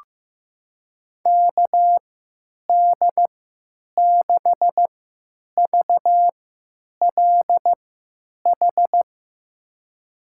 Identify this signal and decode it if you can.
Morse code, heard as a keyed tone, about 15 wpm, spaced as for 8 wpm, 700 Hz. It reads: KD6VLH
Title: USA Call Signs - 6 Characters - 3x Character Spacing - 15wpm